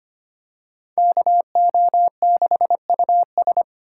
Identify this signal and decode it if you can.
Morse code, heard as a keyed tone, about 25 wpm, 700 Hz. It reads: KO6US
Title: USA Call Signs - 4x Character Spacing - 25wpm